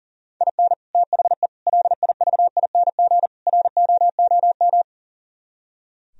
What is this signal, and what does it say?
Morse code, heard as a keyed tone, about 37 wpm, 700 Hz. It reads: IN THE LIVING ROOM
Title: Speed-Racing - Sets of 4 Words 40wpm